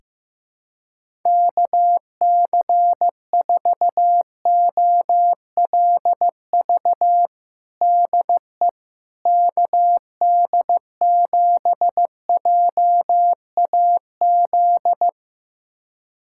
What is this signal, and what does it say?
Morse code, heard as a keyed tone, about 15 wpm, 700 Hz. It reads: KC4OLV DE KD7JAZ
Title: USA Callsign DE Callsign - 6 Characters - 15wpm